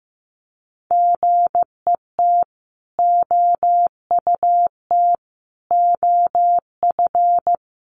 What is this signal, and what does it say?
Morse code, heard as a keyed tone, about 15 wpm, 700 Hz. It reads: GET OUT OF
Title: A Sound Of Thunder 15wpm